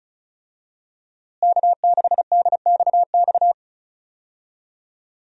Morse code, heard as a keyed tone, about 36 wpm, 700 Hz. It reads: K6DXX
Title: USA Call Signs - 5 Characters - Repeated 3x - 35wpm